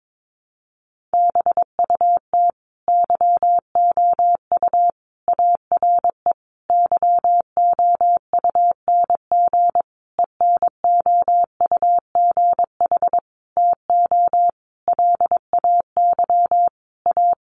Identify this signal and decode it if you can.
Morse code, heard as a keyed tone, about 22 wpm, 700 Hz. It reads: BUT YOU ARE YOUNG ENOUGH TO LAY A
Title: Sentences from Top 1200 Words 22wpm